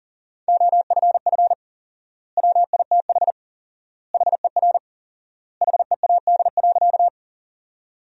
Morse code, heard as a keyed tone, about 40 wpm, 700 Hz. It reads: OFF WITH HER HEAD.
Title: 4x Word-Spacing - Sentences from Top 300 Words - 40wpm